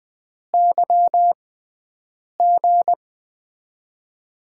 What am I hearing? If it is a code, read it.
Morse code, heard as a keyed tone, about 20 wpm, spaced as for 7 wpm, 700 Hz. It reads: YG
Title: USA Call Signs - 6x Character Spacing - 20wpm